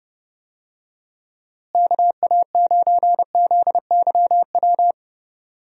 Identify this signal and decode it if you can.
Morse code, heard as a keyed tone, about 30 wpm, 700 Hz. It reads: KA9ZYW
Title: USA Call Signs - 6 Characters - Repeated 3x - 30wpm